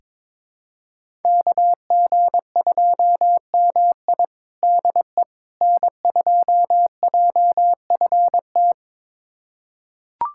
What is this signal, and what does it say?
Morse code, heard as a keyed tone, about 22 wpm, 700 Hz. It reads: KG2MI DE N2JFT
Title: USA Callsign DE Callsign 22wpm